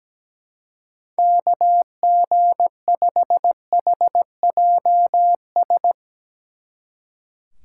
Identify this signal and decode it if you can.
Morse code, heard as a keyed tone, about 18 wpm, 700 Hz. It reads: KG5HJS
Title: Speed Racing - USA 2x3 Call Signs 17wpm